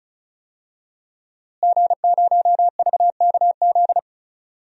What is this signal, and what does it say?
Morse code, heard as a keyed tone, about 34 wpm, 700 Hz. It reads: G0VKZ
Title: DXCC-100 Call Signs - 5 Characters - Repeated 2x - 35wpm